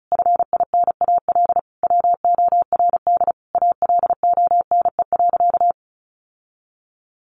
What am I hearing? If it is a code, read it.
Morse code, heard as a keyed tone, about 34 wpm, 700 Hz. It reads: FINAL WORD ALONE.
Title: Sentences from Top 1000 Words 35wpm